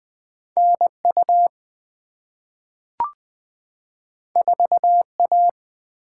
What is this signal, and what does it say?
Morse code, heard as a keyed tone, about 20 wpm, 700 Hz. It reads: NU 4A
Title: Two Letter-Number Combinations - 20wpm